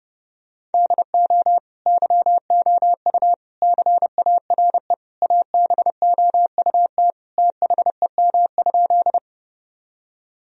Morse code, heard as a keyed tone, about 30 wpm, 700 Hz. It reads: DO YOU CARE ABOUT THEM?
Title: Sentences from Top 1000 Words - Review Part 1 - 30wpm